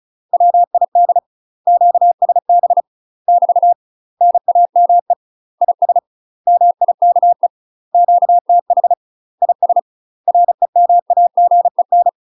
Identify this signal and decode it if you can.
Morse code, heard as a keyed tone, about 33 wpm, 700 Hz. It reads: WID QSB <BT> NAME IS MIKE QTH IS REMAGEN
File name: UK MASTER QSOs TRAINING / 35WPM / UK MASTER QSOs 0002